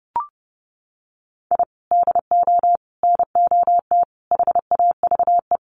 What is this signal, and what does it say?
Morse code, heard as a keyed tone, about 30 wpm, 700 Hz. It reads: I DO NOT HAVE
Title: Sets of 5 Words 30wpm